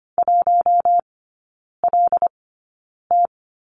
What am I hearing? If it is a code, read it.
Morse code, heard as a keyed tone, about 25 wpm, spaced as for 9 wpm, 700 Hz. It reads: 1LT
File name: Lesson 14e - Number 2 - Three Characters 25wpm